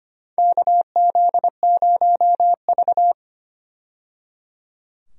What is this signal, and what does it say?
Morse code, heard as a keyed tone, about 25 wpm, 700 Hz. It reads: KZ0V
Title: USA Call Signs - 4 Characters - 25wpm